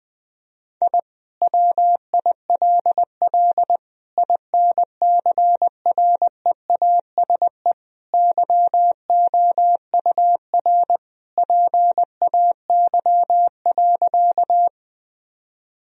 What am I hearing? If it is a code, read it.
Morse code, heard as a keyed tone, about 20 wpm, 700 Hz. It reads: I WILL INCREASE YOUR PAY.
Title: Sentences from Top 1000 Words - Review Part 4 - 20wpm